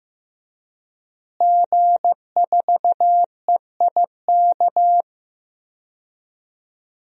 Morse code, heard as a keyed tone, about 15 wpm, 700 Hz. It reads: G4EIK
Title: DXCC-100 Call Signs - 5 Characters - Repeated 2x - 15wpm